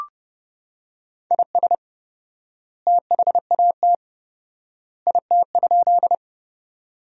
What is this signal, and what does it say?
Morse code, heard as a keyed tone, about 30 wpm, 700 Hz. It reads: IS THAT IT?
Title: 4x Word-Spacing - Sentences from Top 100 Words - 30wpm